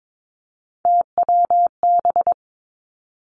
Morse code, heard as a keyed tone, about 22 wpm, 700 Hz. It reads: TWB
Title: Three Challenging Letter-Number Combinations 22wpm